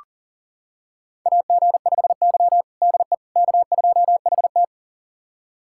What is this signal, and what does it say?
Morse code, heard as a keyed tone, about 37 wpm, 700 Hz. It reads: AG5Y DE K2HT
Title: USA Callsign DE Callsign - 4 Characters - 40wpm